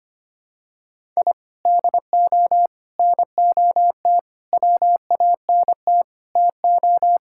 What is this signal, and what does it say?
Morse code, heard as a keyed tone, about 25 wpm, 700 Hz. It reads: I DO NOT WANT TO
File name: practice / 25wpm / Sentences with 4-Word Phrases - 25wpm